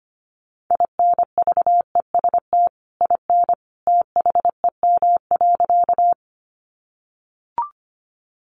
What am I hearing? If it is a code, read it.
Morse code, heard as a keyed tone, about 25 wpm, 700 Hz. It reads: INVEST IN THEM.
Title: Sentences from Top 1100 Words 25wpm